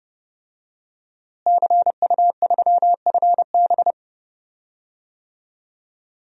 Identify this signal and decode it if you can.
Morse code, heard as a keyed tone, about 30 wpm, 700 Hz. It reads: CU3FB
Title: DXCC-100 Call Signs - 5 Characters - Repeated 2x - 30wpm